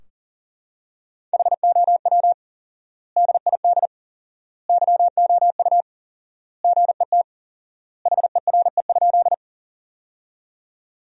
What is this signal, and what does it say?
Morse code, heard as a keyed tone, about 40 wpm, 700 Hz. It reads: HOW DID YOU GET HERE?
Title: 4x Word-Spacing - Sentences from Top 200 Words - 40wpm